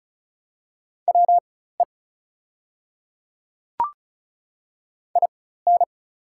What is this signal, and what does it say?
Morse code, heard as a keyed tone, about 36 wpm, spaced as for 16 wpm, 700 Hz. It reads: WE IN
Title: 4x Character Spacing - Top 100 Words - 35wpm